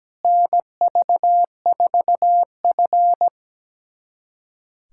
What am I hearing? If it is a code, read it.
Morse code, heard as a keyed tone, about 17 wpm, 700 Hz. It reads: NV4F
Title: USA Callsign DE Callsign - 4 Characters - 17wpm